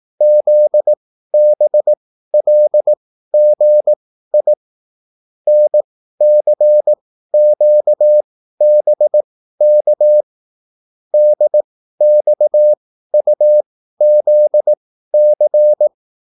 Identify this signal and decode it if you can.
Morse code, heard as a keyed tone, about 18 wpm, spaced as for 13 wpm, 600 Hz. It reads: ZBLGI NCQBK DXUZC